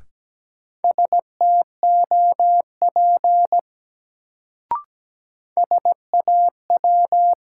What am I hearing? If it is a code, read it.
Morse code, heard as a keyed tone, about 17 wpm, 700 Hz. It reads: STOP SAW
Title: Sentences from Top 300 Words - New Word Drill - 17wpm